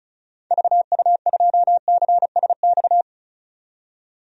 Morse code, heard as a keyed tone, about 36 wpm, 700 Hz. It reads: VU2CSX
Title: DXCC-100 Call Signs - 6 Characters - 35wpm